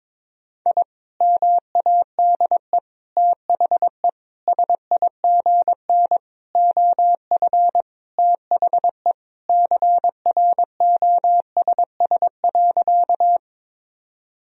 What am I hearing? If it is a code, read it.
Morse code, heard as a keyed tone, about 22 wpm, 700 Hz. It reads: I MADE THE SIGN OF THE CROSS.
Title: Sentences from Top 1000 Words - Review Part 1 - 22wpm